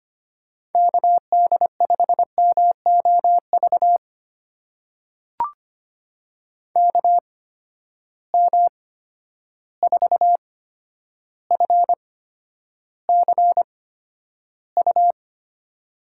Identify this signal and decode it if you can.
Morse code, heard as a keyed tone, about 25 wpm, spaced as for 11 wpm, 700 Hz. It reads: KD5MOV KM4FCU
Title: USA Call Signs - 6 Characters - 8x Character Spacing - 25wpm